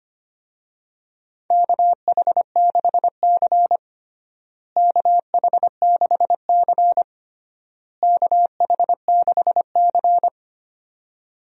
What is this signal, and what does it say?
Morse code, heard as a keyed tone, about 25 wpm, 700 Hz. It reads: KH6C KH6C KH6C
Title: DXCC-100 Call Signs - 4 Characters - Repeated 3x - 25wpm